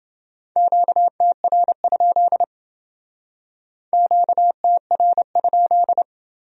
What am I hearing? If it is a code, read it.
Morse code, heard as a keyed tone, about 28 wpm, 700 Hz. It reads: QTR? QTR?